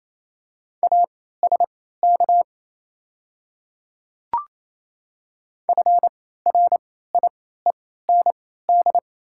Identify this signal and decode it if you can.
Morse code, heard as a keyed tone, about 28 wpm, spaced as for 16 wpm, 700 Hz. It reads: ASK FRIEND
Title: 3x Character Spacing - Top 500 Words - 28wpm